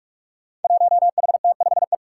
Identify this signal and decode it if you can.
Morse code, heard as a keyed tone, about 44 wpm, 700 Hz. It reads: 1HT5E
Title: Five Challenging Letter-Number Combinations 45wpm